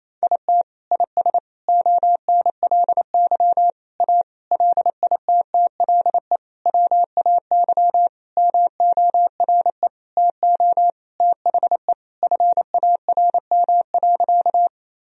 Morse code, heard as a keyed tone, about 28 wpm, 700 Hz. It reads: IT IS ONLY A LITTLE WAY MORE TO THE FARM.
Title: Sentences from Top 1000 Words - Review Part 2 - 28wpm